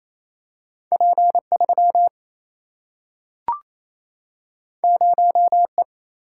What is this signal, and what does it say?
Morse code, heard as a keyed tone, about 28 wpm, 700 Hz. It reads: P3 0E
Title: Single-Letter Single-Number 28wpm